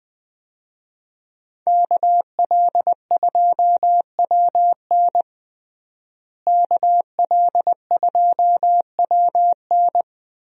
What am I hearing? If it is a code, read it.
Morse code, heard as a keyed tone, about 20 wpm, 700 Hz. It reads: KL2WN KL2WN
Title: USA Call Signs - 5 Characters - Repeated 3x - 20wpm